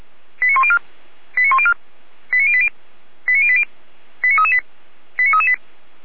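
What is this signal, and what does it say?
Five-tone selective call (ZVEI), heard as quick runs of five tones, tones of about 70 ms. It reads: 89104 89104 89080 89080 89338 89338
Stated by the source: Five-tone ZVEI selective call received by SDR.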